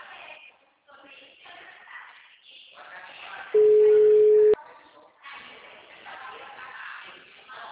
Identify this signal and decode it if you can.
Ringback tone (Europe, 425 Hz), with speech.
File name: out- -2009- - - .68216